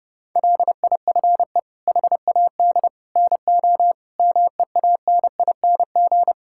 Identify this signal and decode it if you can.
Morse code, heard as a keyed tone, about 30 wpm, 700 Hz. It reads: LIFE HAD NO MEANING
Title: Sentences from Top 1000 Words - Review Part 6 - 30wpm